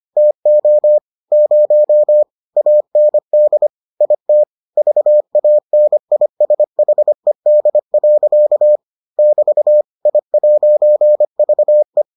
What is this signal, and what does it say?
Morse code, heard as a keyed tone, about 25 wpm, 600 Hz. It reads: TO 0 AND IT VANISHED. <BT> I'VE